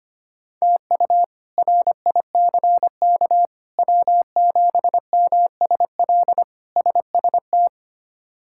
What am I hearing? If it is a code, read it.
Morse code, heard as a keyed tone, about 25 wpm, 700 Hz. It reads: TU RICK W7MSL SST
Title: K1USN SST QSO Practice - 25wpm